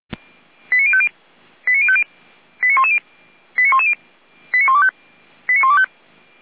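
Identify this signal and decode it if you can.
Five-tone selective call (ZVEI), heard as quick runs of five tones, tones of about 70 ms. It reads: 89050 89050 89110 89125
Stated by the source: ZVEI five-tone call 89140, received via SDR, feuerwehr-orlamuende.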